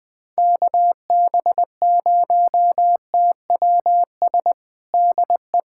Morse code, heard as a keyed tone, about 21 wpm, 700 Hz. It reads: KB0TWS DE
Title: USA Callsign DE Callsign - 6 Characters - 20wpm